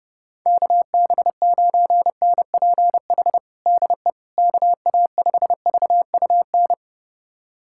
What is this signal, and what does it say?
Morse code, heard as a keyed tone, about 29 wpm, 700 Hz. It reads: KB9NPH DE KA5VUN
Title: USA Callsign DE Callsign 30wpm